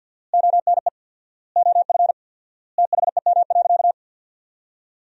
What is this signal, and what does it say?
Morse code, heard as a keyed tone, about 50 wpm, 700 Hz. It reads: ONE OF THEM.
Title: 4x Word-Spacing - Sentences from Top 300 Words - 50wpm